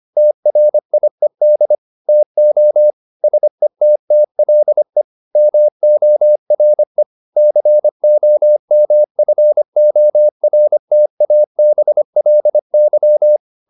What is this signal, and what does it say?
Morse code, heard as a keyed tone, about 25 wpm, 600 Hz. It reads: TRIED TO SETTLE MORE COMFORTABLY